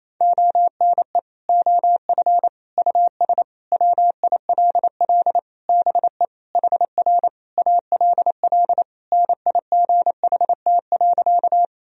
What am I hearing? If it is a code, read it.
Morse code, heard as a keyed tone, about 28 wpm, 700 Hz. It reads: ONE OF US WILL BE HR ALL NIGHT.
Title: Sentences from Top 300 Words and Abbreviations 28wpm